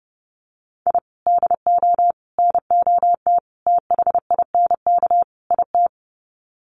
Morse code, heard as a keyed tone, about 30 wpm, 700 Hz. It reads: I DO NOT THINK IT
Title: Sets of 5 Words 30wpm